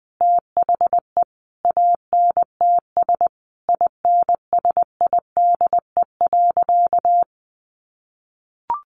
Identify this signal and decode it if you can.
Morse code, heard as a keyed tone, about 20 wpm, 700 Hz. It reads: THE ANTS INSIDE.
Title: Additional Easy Phrases 20wpm